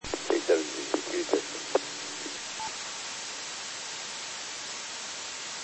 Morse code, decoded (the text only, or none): none